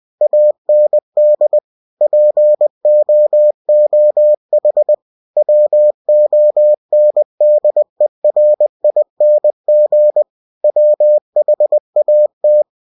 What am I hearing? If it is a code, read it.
Morse code, heard as a keyed tone, about 20 wpm, 600 Hz. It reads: AND POOH WONDERING WHAT
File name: cw-books / winnie-the-pooh / winnie-20wpm-page-0022